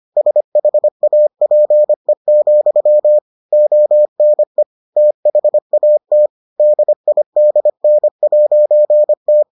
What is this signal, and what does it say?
Morse code, heard as a keyed tone, about 25 wpm, 600 Hz. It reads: SHAPE, ONE THAT DIDN'T